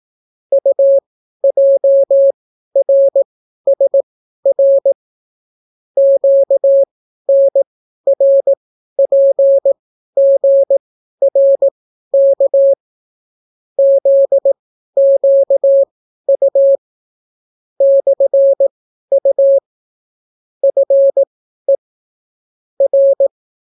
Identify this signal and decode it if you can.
Morse code, heard as a keyed tone, about 18 wpm, spaced as for 12 wpm, 550 Hz. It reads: UJRSR QNRPGRK ZQU /U FE R